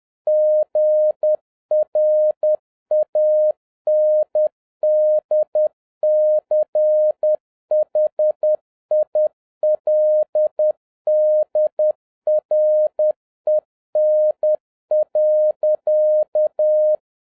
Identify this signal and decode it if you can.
Morse code, heard as a keyed tone, about 10 wpm, 610 Hz. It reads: GRANDCHILDREN.